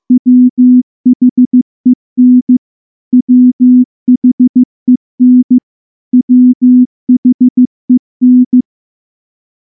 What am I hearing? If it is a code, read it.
Morse code, heard as a keyed tone, about 14 wpm, 250 Hz. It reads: WHEN WHEN WHEN
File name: morse / morse-training-15WPM